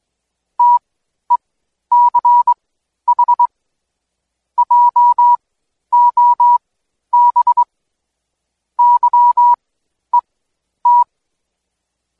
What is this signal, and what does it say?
Morse code, heard as a keyed tone, about 21 wpm, spaced as for 11 wpm, 970 Hz. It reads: TECH JOB YET
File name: Code Quick / AudioDownld2 / Seg120-Day 25-DayDrill 25